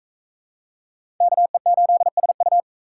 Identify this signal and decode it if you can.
Morse code, heard as a keyed tone, about 36 wpm, 700 Hz. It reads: KE8SU